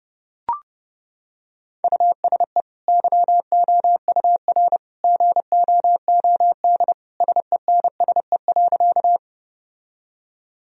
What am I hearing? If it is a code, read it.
Morse code, heard as a keyed tone, about 30 wpm, 700 Hz. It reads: USE YOUR GOOD SENSE.
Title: Sentences from Top 1000 Words - Review Part 4 - 30wpm